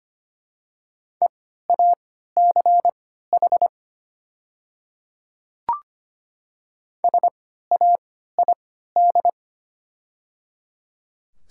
Morse code, heard as a keyed tone, about 25 wpm, spaced as for 14 wpm, 700 Hz. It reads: EACH SAID
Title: 3x Character Spacing - Top 100 Words - 25wpm